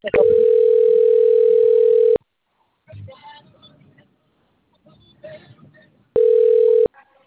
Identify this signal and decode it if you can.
Ringback tone (North America), 440 + 480 Hz.
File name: out- - - - - .14858